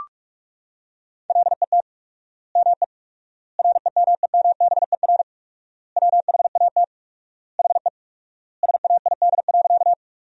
Morse code, heard as a keyed tone, about 46 wpm, 700 Hz. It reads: LET ME REMEMBER WHAT HE SAID.
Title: 4x Word-Spacing - Sentences from Top 400 Words - 45wpm